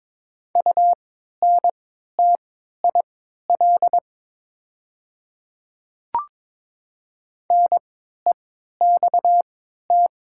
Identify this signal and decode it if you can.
Morse code, heard as a keyed tone, about 22 wpm, spaced as for 12 wpm, 700 Hz. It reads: UNTIL NEXT
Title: 3x Character Spacing - Top 300 Words - 22wpm